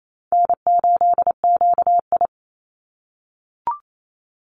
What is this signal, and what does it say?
Morse code, heard as a keyed tone, about 28 wpm, 700 Hz. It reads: N8QI
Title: USA Call Signs 28wpm